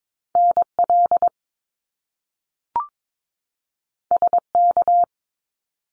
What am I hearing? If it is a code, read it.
Morse code, heard as a keyed tone, about 22 wpm, 700 Hz. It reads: NL SK
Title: Canadian Provinces and Territories 22wpm